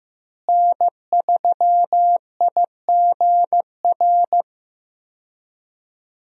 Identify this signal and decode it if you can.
Morse code, heard as a keyed tone, about 16 wpm, 700 Hz. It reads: N3IGR
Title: USA Call Signs - 5 Characters - Repeated 3x - 15wpm